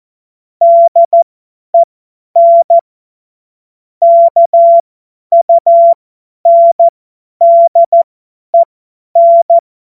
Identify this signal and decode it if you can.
Morse code, heard as a keyed tone, about 14 wpm, spaced as for 10 wpm, 680 Hz. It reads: DEN KUNDEN
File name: Morsekurs / Witze / Witz 16 Januar 2024 50 F70